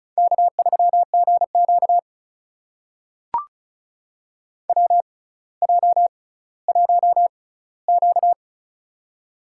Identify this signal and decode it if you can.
Morse code, heard as a keyed tone, about 36 wpm, spaced as for 18 wpm, 700 Hz. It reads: K3GQ WJ1Q
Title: USA Call Signs - 4 Characters - 6x Character Spacing - 35wpm